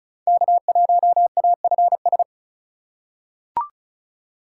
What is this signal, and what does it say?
Morse code, heard as a keyed tone, about 36 wpm, 700 Hz. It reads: K1AFS
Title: USA Call Signs - 5 Characters - 2x Character Spacing - 35wpm